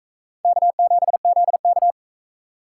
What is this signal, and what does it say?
Morse code, heard as a keyed tone, about 41 wpm, 700 Hz. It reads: K7ZK